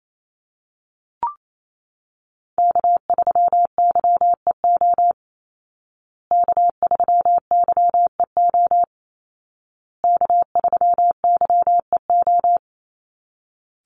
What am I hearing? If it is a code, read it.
Morse code, heard as a keyed tone, about 28 wpm, 700 Hz. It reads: K3YEO K3YEO K3YEO
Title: Call Signs Repeated 2x - 28wpm